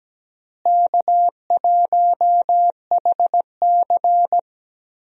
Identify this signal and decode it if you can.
Morse code, heard as a keyed tone, about 17 wpm, 700 Hz. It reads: K1HC
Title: USA Call Signs - 4 Characters - Repeated 2x - 17wpm